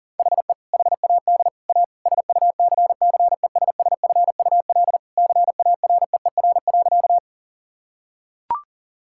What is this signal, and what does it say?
Morse code, heard as a keyed tone, about 37 wpm, 700 Hz. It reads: HE HAD A SUCCESSFUL CAREER.
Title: Sentences from Top 1000 Words - Review Part 6 - 40wpm